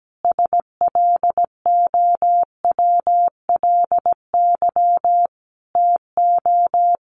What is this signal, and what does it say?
Morse code, heard as a keyed tone, about 17 wpm, 700 Hz. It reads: SLOWLY TO
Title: A Sound Of Thunder 17wpm